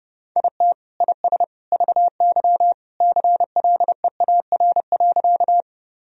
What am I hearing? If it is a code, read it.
Morse code, heard as a keyed tone, about 30 wpm, 700 Hz. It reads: IT IS VY CLEAR.
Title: Sentences from Top 500 Words and Abbreviations 30wpm